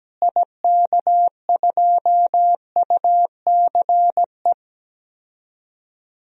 Morse code, heard as a keyed tone, about 17 wpm, 700 Hz. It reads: IK2UCE
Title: DXCC-100 Call Signs - 6 Characters - Repeated 3x - 17wpm